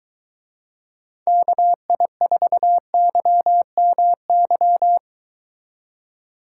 Morse code, heard as a keyed tone, about 23 wpm, 700 Hz. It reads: KI4YMY